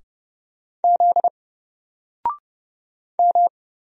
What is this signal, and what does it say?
Morse code, heard as a keyed tone, about 30 wpm, 700 Hz. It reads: Z M
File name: Single-Letter-Number-30wpm